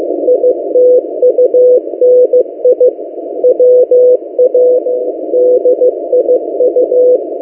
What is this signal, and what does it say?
Morse code, heard as a keyed tone, about 15 wpm, 500 Hz. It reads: ITUNI WADIU
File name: M08a- -1355utc-8096khz